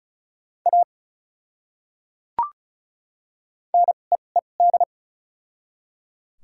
Morse code, heard as a keyed tone, about 34 wpm, spaced as for 25 wpm, 700 Hz. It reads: A NEED
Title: 2x Character Spacing - Top 300 Words - 35wpm